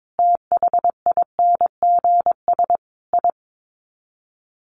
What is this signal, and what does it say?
Morse code, heard as a keyed tone, about 22 wpm, 700 Hz. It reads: THINGS I
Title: Sets of 5 Words 22wpm